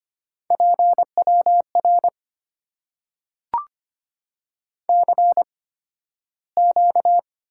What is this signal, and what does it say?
Morse code, heard as a keyed tone, about 25 wpm, spaced as for 13 wpm, 690 Hz. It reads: PWR CQ
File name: Top 90 QSO Elements - 8x Character Spacing - 25wpm